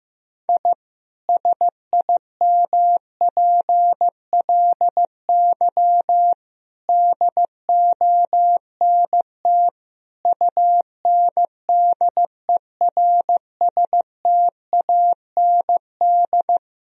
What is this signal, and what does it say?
Morse code, heard as a keyed tone, about 15 wpm, 700 Hz. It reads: I SIMPLY DONT UNDERSTAND
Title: Sentences from Top 1000 Words - Review Part 4 - 15wpm